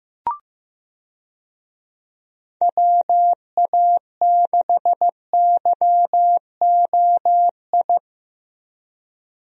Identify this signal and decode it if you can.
Morse code, heard as a keyed tone, about 15 wpm, 700 Hz. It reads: WA6YOI
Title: USA Call Signs - 6 Characters - Repeated 3x - 15wpm